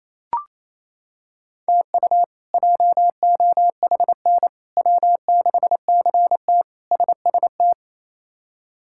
Morse code, heard as a keyed tone, about 28 wpm, 700 Hz. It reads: TU JOHN W6CT SST
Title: K1USN SST QSO Practice - 28wpm